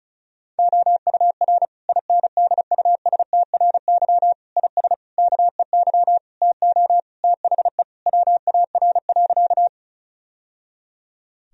Morse code, heard as a keyed tone, about 36 wpm, 700 Hz. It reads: OUR INDUSTRY IS KEY TO THE WAR.
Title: Sentences from Top 1000 Words - Review Part 4 - 35wpm